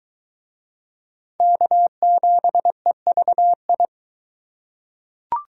Morse code, heard as a keyed tone, about 22 wpm, 700 Hz. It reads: K7EVI